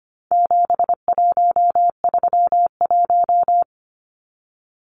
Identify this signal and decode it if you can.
Morse code, heard as a keyed tone, about 25 wpm, 700 Hz. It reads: 7131
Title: Numbers - 4-Digit 25wpm